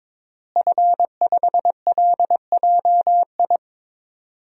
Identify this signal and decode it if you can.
Morse code, heard as a keyed tone, about 22 wpm, 700 Hz. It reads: F5LJI
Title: DXCC-100 Call Signs - 5 Characters - Repeated 3x - 22wpm